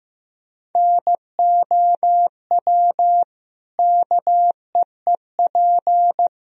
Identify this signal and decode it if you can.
Morse code, heard as a keyed tone, about 15 wpm, 700 Hz. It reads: NOW KEEP
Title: Sentences from Top 1000 Words - Review Part 2 - 15wpm